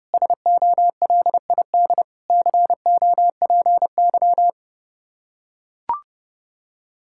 Morse code, heard as a keyed tone, about 30 wpm, 700 Hz. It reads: SOLID COPY
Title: Practice QSO - Exchange 3 - More Info - 30wpm